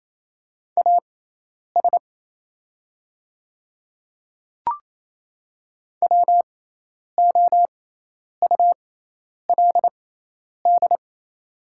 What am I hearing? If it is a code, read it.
Morse code, heard as a keyed tone, about 27 wpm, spaced as for 10 wpm, 690 Hz. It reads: AS WOULD
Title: 6x Character Spacing - Top 100 Words - 28wpm